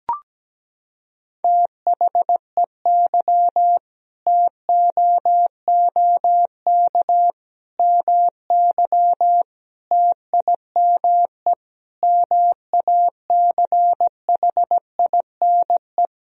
Morse code, heard as a keyed tone, about 17 wpm, 700 Hz. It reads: THEY TOOK MY TIME MACHINE
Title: Sentences from Top 1000 Words - Review Part 2 - 17wpm